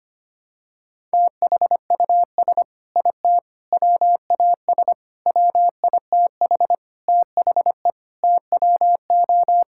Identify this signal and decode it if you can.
Morse code, heard as a keyed tone, about 25 wpm, 700 Hz. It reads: THUS IT WAS WITH THE TWO
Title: Sentences from Top 1000 Words - Review Part 5 - 25wpm